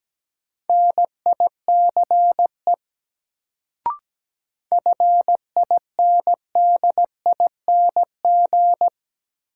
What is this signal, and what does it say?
Morse code, heard as a keyed tone, about 17 wpm, 700 Hz. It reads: NICE FINDING
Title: Sentences from Top 1100 Words - New Word Drill Alt - 17wpm